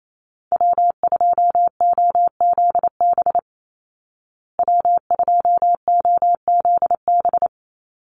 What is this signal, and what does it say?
Morse code, heard as a keyed tone, about 28 wpm, 700 Hz. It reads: W2OZB W2OZB
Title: Call Signs Repeated 3x - 28wpm